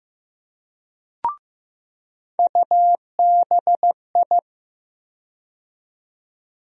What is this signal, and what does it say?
Morse code, heard as a keyed tone, about 15 wpm, 700 Hz. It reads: UBI
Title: USA Call Sign Postfixes - 3 Letters 15wpm